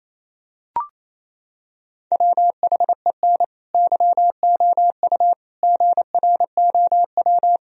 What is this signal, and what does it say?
Morse code, heard as a keyed tone, about 28 wpm, 700 Hz. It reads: WHEN YOU GROW
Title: Sentences from Top 1000 Words - Review Part 1 - 28wpm